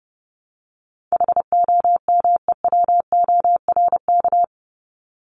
Morse code, heard as a keyed tone, about 30 wpm, 700 Hz. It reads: HOMEWORK
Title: Top 1501-1750 Words 30wpm